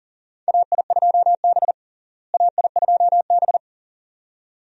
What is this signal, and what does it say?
Morse code, heard as a keyed tone, about 40 wpm, 700 Hz. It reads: AI2B AI2B
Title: USA Call Signs - 4 Characters - Repeated 2x - 40wpm